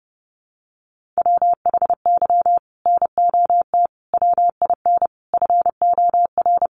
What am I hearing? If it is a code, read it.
Morse code, heard as a keyed tone, about 30 wpm, 700 Hz. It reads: WHY NOT WIN FOR
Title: Sentences from Top 1500 Words 30wpm